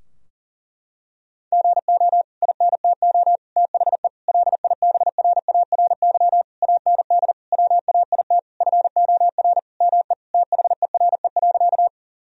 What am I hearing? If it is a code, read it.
Morse code, heard as a keyed tone, about 40 wpm, 700 Hz. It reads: GO INTO THE LIBRARY AND WAIT FOR ME THERE.
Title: Sentences from Top 1000 Words - Review Part 4 - 40wpm